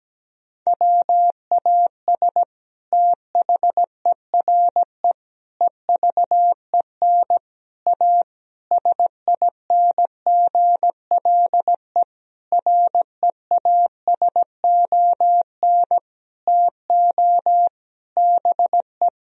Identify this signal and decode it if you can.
Morse code, heard as a keyed tone, about 17 wpm, 700 Hz. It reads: WAS THERE EVEN A SINGLE REASON TO BE
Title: Sentences from Top 1000 Words - Review Part 4 - 17wpm